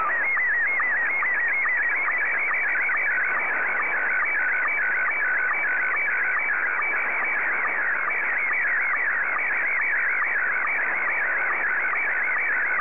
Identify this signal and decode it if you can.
SSTV (slow-scan television), Scottie 1.